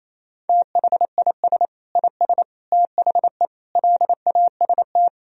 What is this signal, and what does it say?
Morse code, heard as a keyed tone, about 28 wpm, 700 Hz. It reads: THIS IS THE LAST
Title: Sentences from Top 1000 Words - Review Part 4 - 28wpm